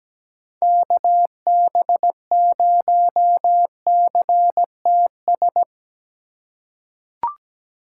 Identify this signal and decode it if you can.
Morse code, heard as a keyed tone, about 17 wpm, 700 Hz. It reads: KB0CTS
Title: USA Call Signs - 6 Characters - 2x Character Spacing - 17wpm